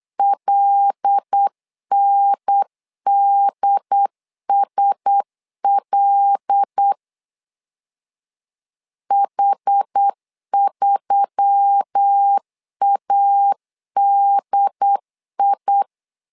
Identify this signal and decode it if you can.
Morse code, heard as a keyed tone, about 8 wpm, 800 Hz. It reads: LNDSL H3ADI
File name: LEZIONI / 5 LBVC / 5 L 8wpm 2831